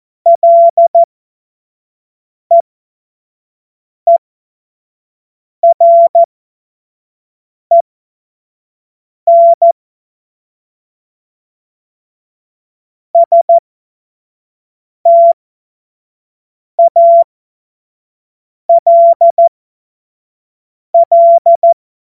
Morse code, heard as a keyed tone, about 14 wpm, spaced as for 5 wpm, 680 Hz. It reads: LEEREN STALL
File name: Witz-19-August-2025-25-F70